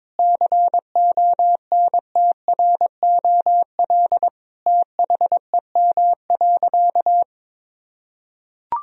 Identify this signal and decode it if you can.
Morse code, heard as a keyed tone, about 22 wpm, 700 Hz. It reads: CONTROL THEM.
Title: Sentences from Top 1000 Words - Review Part 2 - 22wpm